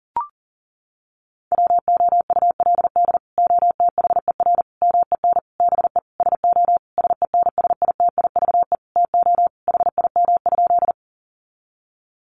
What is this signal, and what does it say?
Morse code, heard as a keyed tone, about 40 wpm, 700 Hz. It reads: WOULD OTHER MEN BE SO SENSITIVE TO HIM?
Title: Sentences from Top 1400 Words 40wpm